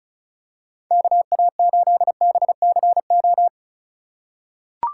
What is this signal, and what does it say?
Morse code, heard as a keyed tone, about 36 wpm, 700 Hz. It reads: KA8BCO
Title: USA Call Signs - 2x Character Spacing - 35wpm